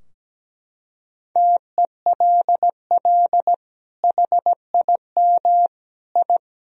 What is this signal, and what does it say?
Morse code, heard as a keyed tone, about 17 wpm, 700 Hz. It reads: TELL HIM I
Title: Sentences from Top 1000 Words - Review Part 2 - 17wpm